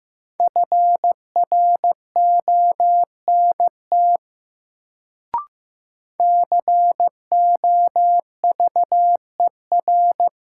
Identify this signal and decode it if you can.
Morse code, heard as a keyed tone, about 15 wpm, 700 Hz. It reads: FRONT COVER
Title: Sentences from Top 500 Words - New Word Drill - 15wpm